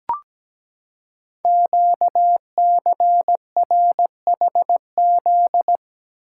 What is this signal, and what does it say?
Morse code, heard as a keyed tone, about 17 wpm, 700 Hz. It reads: QCRHZ
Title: Code Groups - 5 Letters - 17wpm